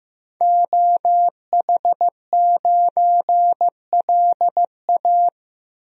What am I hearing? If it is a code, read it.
Morse code, heard as a keyed tone, about 15 wpm, 700 Hz. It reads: OH9LA
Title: DXCC-100 Call Signs - 5 Characters - 15wpm